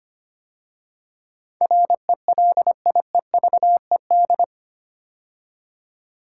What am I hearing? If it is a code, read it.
Morse code, heard as a keyed tone, about 25 wpm, 700 Hz. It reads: RELIEVED
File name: 8 Letter Words - First 3 Letters Given - 25wpm